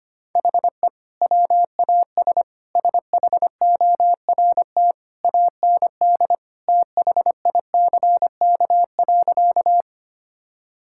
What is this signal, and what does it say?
Morse code, heard as a keyed tone, about 25 wpm, 700 Hz. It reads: HE WAS SHORT AND THICK.
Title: Sentences from Top 1600 Words 25wpm